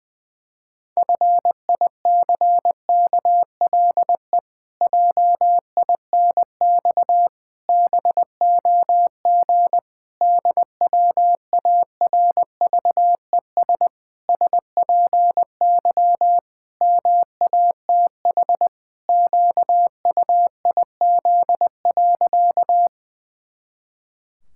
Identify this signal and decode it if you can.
Morse code, heard as a keyed tone, about 20 wpm, 700 Hz. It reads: FICKLE JINX BOG DWARVES SPY MATH QUIZ.
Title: Pangram Challenge v2 - 20wpm